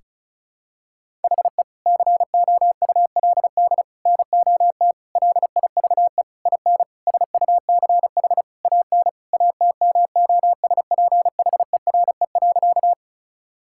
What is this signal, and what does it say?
Morse code, heard as a keyed tone, about 36 wpm, 700 Hz. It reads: HE COULD NOT LIVE IN SUCH AN ATMOSPHERE.
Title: Sentences from Top 1600 Words 35wpm